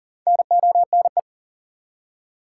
Morse code, heard as a keyed tone, about 40 wpm, 700 Hz. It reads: NONE
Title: Four Letter Words - 40wpm